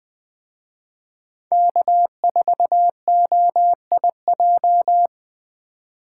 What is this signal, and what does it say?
Morse code, heard as a keyed tone, about 20 wpm, 700 Hz. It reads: K4OIJ